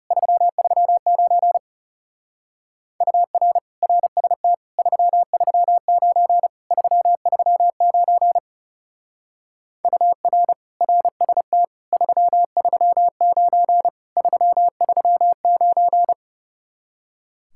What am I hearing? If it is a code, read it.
Morse code, heard as a keyed tone, about 36 wpm, 700 Hz. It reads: 339 UR RST 339 339 UR RST 339 339
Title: Speed-Racing Common QSO Phrases - 30wpm